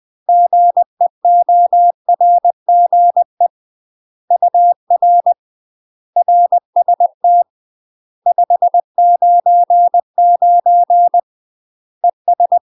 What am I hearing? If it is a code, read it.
Morse code, heard as a keyed tone, about 20 wpm, 700 Hz. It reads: GEORGE UR RST 599 ES
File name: UK MASTER QSOs TRAINING / 20WPM / UK MASTER QSOs 0002